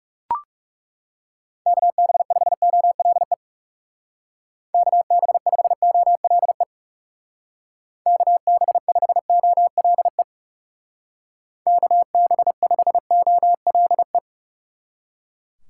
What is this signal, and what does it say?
Morse code, heard as a keyed tone, about 37 wpm, 700 Hz. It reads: KB5OLE KB5OLE KB5OLE KB5OLE
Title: Speed Racing - USA 2x3 Call Signs 30wpm